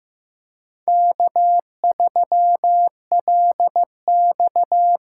Morse code, heard as a keyed tone, about 15 wpm, 700 Hz. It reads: K3LX
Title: USA Call Signs - 4 Characters - 3x Character Spacing - 15wpm